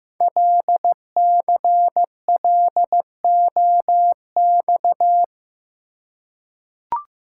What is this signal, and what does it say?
Morse code, heard as a keyed tone, about 15 wpm, 700 Hz. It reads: LCLOX
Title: Code Groups - 5 Letter:Numbers - 15wpm